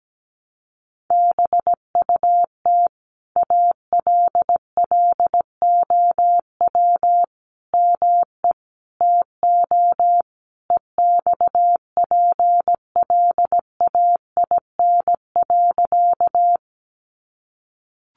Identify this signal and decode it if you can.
Morse code, heard as a keyed tone, about 17 wpm, 700 Hz. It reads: BUT ALLOW ME TO EXPLAIN.
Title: Sentences from Top 1100 Words 17wpm